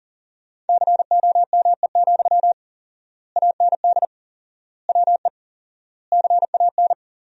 Morse code, heard as a keyed tone, about 40 wpm, 700 Hz. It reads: COME, AND WE CAN
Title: 4x Word-Spacing - Sentences from Top 500 Words - 40wpm